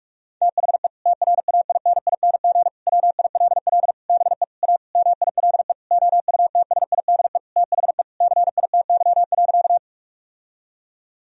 Morse code, heard as a keyed tone, about 48 wpm, 700 Hz. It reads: THE TRAINING WILL BE A MILE OUTSIDE THE CITY.
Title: Sentences from Top 1000 Words - Review Part 4 - 45wpm